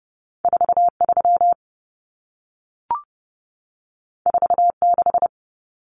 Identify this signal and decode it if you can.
Morse code, heard as a keyed tone, about 30 wpm, 700 Hz. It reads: 43 46
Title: Common Cut Numbers - 2-Digit 30wpm